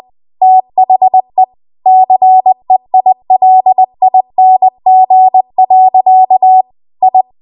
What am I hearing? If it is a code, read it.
Morse code, heard as a keyed tone, about 20 wpm, 750 Hz. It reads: THE CEILING. I
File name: Archive / 20 WPM / 20WPM